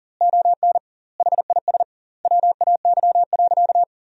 Morse code, heard as a keyed tone, about 37 wpm, 700 Hz. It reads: ON HIS WAY.
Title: 2x Word-Spacing - Sentences from Top 200 Words - 40wpm